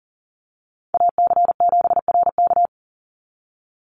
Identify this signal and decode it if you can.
Morse code, heard as a keyed tone, about 40 wpm, 700 Hz. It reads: AC7RK
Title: USA Call Signs 40wpm